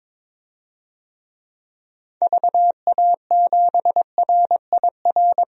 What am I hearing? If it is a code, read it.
Morse code, heard as a keyed tone, about 22 wpm, 700 Hz. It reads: VA7RIR
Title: DXCC-100 Call Signs - 6 Characters - 22wpm